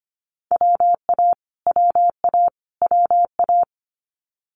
Morse code, heard as a keyed tone, about 25 wpm, 700 Hz. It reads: WA WA WA
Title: Multi-letter Phonemes Repeated 3x 25wpm